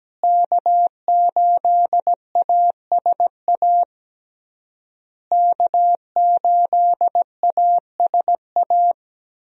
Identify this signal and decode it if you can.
Morse code, heard as a keyed tone, about 17 wpm, 700 Hz. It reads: K8ASA K8ASA
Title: USA Call Signs - 5 Characters - Repeated 3x - 17wpm